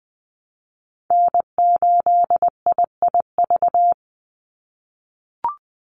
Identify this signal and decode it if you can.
Morse code, heard as a keyed tone, about 20 wpm, 700 Hz. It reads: N8IIV
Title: USA Call Signs 20wpm